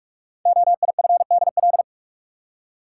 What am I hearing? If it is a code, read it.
Morse code, heard as a keyed tone, about 44 wpm, 700 Hz. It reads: OIFDL
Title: Code Groups - 5 Letters - 45wpm